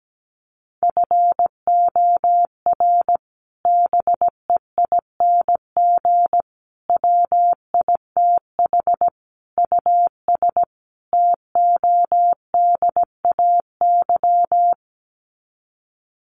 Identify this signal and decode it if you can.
Morse code, heard as a keyed tone, about 17 wpm, 700 Hz. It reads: FOR BEING WITH US TODAY
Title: Sets of 5 Words Encore2 17wpm